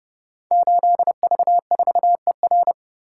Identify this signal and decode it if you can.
Morse code, heard as a keyed tone, about 30 wpm, 700 Hz. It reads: 8V4ER
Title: Code Groups - 5 Letter:Numbers - 30wpm